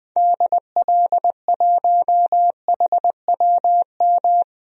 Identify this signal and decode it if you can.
Morse code, heard as a keyed tone, about 20 wpm, 700 Hz. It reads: DL1HWM
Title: DXCC-100 Call Signs - 6 Characters - Repeated 3x - 20wpm